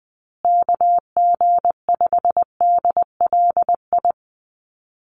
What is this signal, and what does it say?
Morse code, heard as a keyed tone, about 20 wpm, 700 Hz. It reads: KG5DLI
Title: USA Call Signs Repeated 3x 20wpm